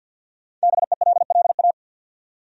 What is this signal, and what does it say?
Morse code, heard as a keyed tone, about 52 wpm, 700 Hz. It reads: BELLA
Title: USA Common Names - 50wpm